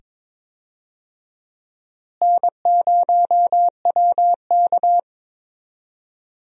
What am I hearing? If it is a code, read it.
Morse code, heard as a keyed tone, about 22 wpm, 700 Hz. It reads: N0WK
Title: USA Call Signs - 4 Characters - Repeated 2x - 22wpm